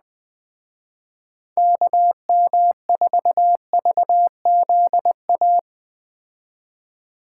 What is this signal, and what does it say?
Morse code, heard as a keyed tone, about 20 wpm, 700 Hz. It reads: KM4VZA